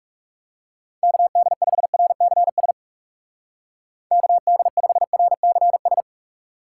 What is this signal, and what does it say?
Morse code, heard as a keyed tone, about 38 wpm, 700 Hz. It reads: KD5RCS KD5RCS
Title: Speed Racing - USA 2x3 Call Signs 30wpm